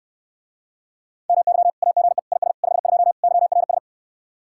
Morse code, heard as a keyed tone, about 69 wpm, 700 Hz. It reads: MY AGE IS 52 YRS